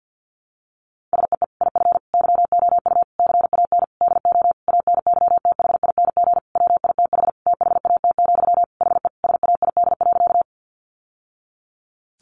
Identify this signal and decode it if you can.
Morse code, heard as a keyed tone, about 50 wpm, 700 Hz. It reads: SEE IF YOU CAN DO ANYTHING WITH THAT, HE SAID.
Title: Sentences from Top 1000 Words 50wpm